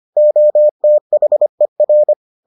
Morse code, heard as a keyed tone, about 25 wpm, 600 Hz. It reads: OTHER